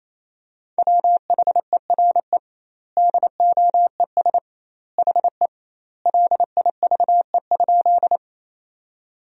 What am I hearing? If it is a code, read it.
Morse code, heard as a keyed tone, about 28 wpm, 700 Hz. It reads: WHERE DOES HE LIVE?
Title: 2x Word-Spacing - Sentences from Top 200 Words - 28wpm